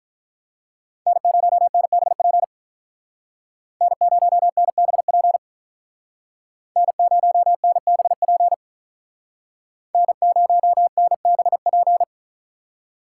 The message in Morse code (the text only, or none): N0NBP N0NBP N0NBP N0NBP